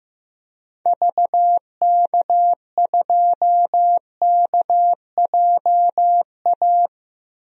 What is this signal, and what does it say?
Morse code, heard as a keyed tone, about 15 wpm, 700 Hz. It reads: VK2KJA
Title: DXCC-100 Call Signs - 6 Characters - 15wpm